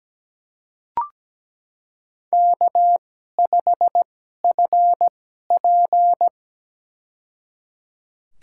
Morse code, heard as a keyed tone, about 17 wpm, spaced as for 12 wpm, 700 Hz. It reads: K5FP
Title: USA Call Signs - 4 Characters - 2x Character Spacing - 17wpm